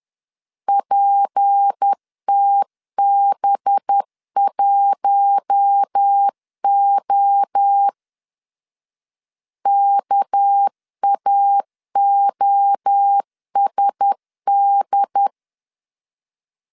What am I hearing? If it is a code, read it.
Morse code, heard as a keyed tone, about 10 wpm, 800 Hz. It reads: PTB1O KAOSD
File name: LEZIONI / 6 PKX0 / 6 PK 10wpm 2436